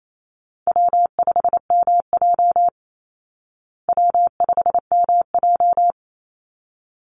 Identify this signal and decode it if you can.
Morse code, heard as a keyed tone, about 28 wpm, 700 Hz. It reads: W5MJ W5MJ
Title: Easy Call Signs Repeated 2x - 28wpm